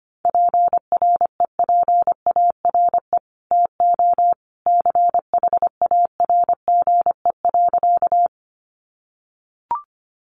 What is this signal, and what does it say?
Morse code, heard as a keyed tone, about 25 wpm, 700 Hz. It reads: PREPARE TO CHARGE.
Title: Sentences from Top 900 Words 25wpm